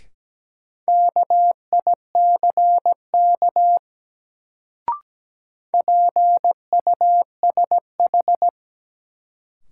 Morse code, heard as a keyed tone, about 17 wpm, 700 Hz. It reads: KICK PUSH
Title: Sentences from Top 1300 Words - New Word Drill Alt - 17wpm